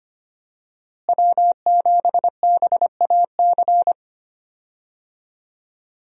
Morse code, heard as a keyed tone, about 25 wpm, 700 Hz. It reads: W7BAC
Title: USA Call Signs - 5 Characters - Repeated 3x - 25wpm